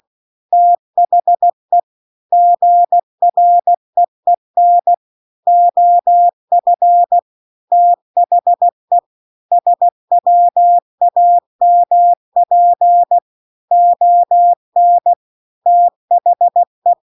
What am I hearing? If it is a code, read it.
Morse code, heard as a keyed tone, about 16 wpm, 700 Hz. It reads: THE GREEN OF THE SWAMP ON THE
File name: 16WPM